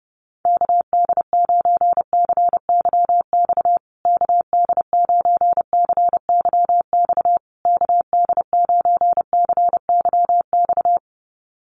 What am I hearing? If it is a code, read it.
Morse code, heard as a keyed tone, about 30 wpm, 700 Hz. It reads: KD9CYX KD9CYX KD9CYX
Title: USA Call Signs Repeated 3x 30wpm